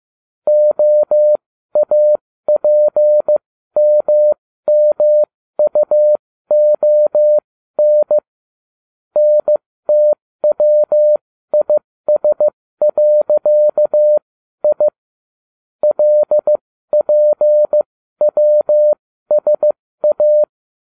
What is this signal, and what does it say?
Morse code, heard as a keyed tone, about 15 wpm, spaced as for 13 wpm, 600 Hz. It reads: OAPMMUON NTWIS.I LPWSA